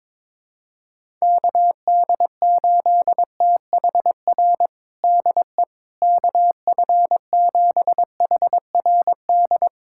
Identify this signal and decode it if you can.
Morse code, heard as a keyed tone, about 21 wpm, 700 Hz. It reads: KD8THR DE KF7HRD
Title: USA Callsign DE Callsign - 6 Characters - 22wpm